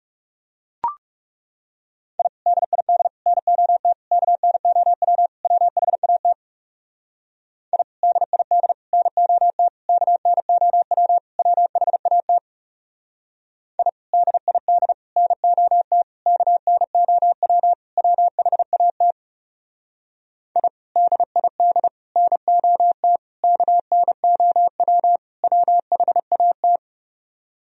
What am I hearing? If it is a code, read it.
Morse code, heard as a keyed tone, about 36 wpm, 700 Hz. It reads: I DID NOT KNOW WHAT I DID NOT KNOW WHAT I DID NOT KNOW WHAT I DID NOT KNOW WHAT
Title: Speed-Racing - Sets of 5 Words 30wpm